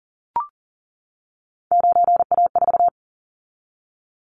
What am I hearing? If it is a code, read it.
Morse code, heard as a keyed tone, about 40 wpm, 700 Hz. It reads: 9A4
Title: Callsign Prefixes for Top 100 DXCC - 40wpm